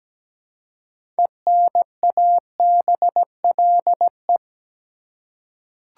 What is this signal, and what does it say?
Morse code, heard as a keyed tone, about 17 wpm, 700 Hz. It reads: ENABLE
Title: Six Letter Words - 17wpm